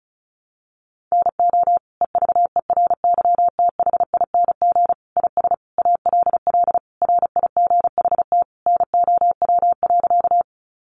Morse code, heard as a keyed tone, about 34 wpm, 700 Hz. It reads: NO EVERYTHING IS ALL RIGHT NOW.
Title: Sentences from Top 700 Words 35wpm